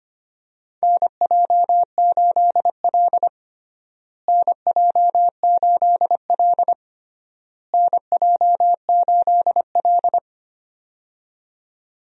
Morse code, heard as a keyed tone, about 25 wpm, 700 Hz. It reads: NJ8L NJ8L NJ8L
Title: USA Call Signs - 4 Characters - Repeated 3x - 25wpm